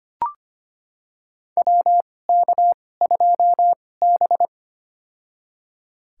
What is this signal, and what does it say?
Morse code, heard as a keyed tone, about 25 wpm, spaced as for 18 wpm, 700 Hz. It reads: WK2B
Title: USA Call Signs - 4 Characters - 2x Character Spacing - 25wpm